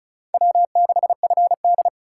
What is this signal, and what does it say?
Morse code, heard as a keyed tone, about 45 wpm, 700 Hz. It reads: W6FD